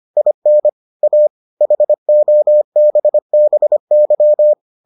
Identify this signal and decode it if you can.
Morse code, heard as a keyed tone, about 25 wpm, 600 Hz. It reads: IN A HOBBY